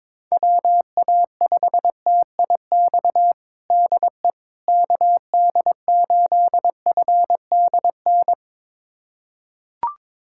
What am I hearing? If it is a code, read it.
Morse code, heard as a keyed tone, about 22 wpm, 700 Hz. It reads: WA5TIX DE KD8FDN
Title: USA Callsign DE Callsign - 6 Characters - 22wpm